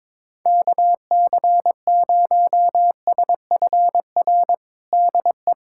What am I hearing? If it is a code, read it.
Morse code, heard as a keyed tone, about 22 wpm, 700 Hz. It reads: KC0SFR DE
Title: USA Callsign DE Callsign - 6 Characters - 22wpm